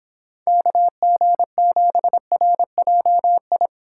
Morse code, heard as a keyed tone, about 26 wpm, 700 Hz. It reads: KG7RJI